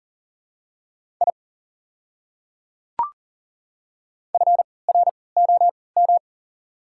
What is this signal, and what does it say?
Morse code, heard as a keyed tone, about 40 wpm, spaced as for 23 wpm, 690 Hz. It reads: I FROM
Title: 3x Character Spacing - Top 300 Words - 40wpm